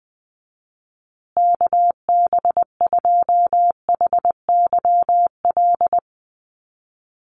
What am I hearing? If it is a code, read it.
Morse code, heard as a keyed tone, about 20 wpm, 700 Hz. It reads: KB2HYL
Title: USA Call Signs Repeated 2x 20wpm